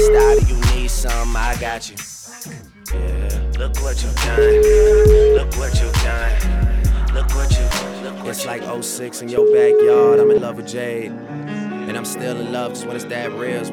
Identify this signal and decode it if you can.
Ringback tone (Europe, 425 Hz), with speech.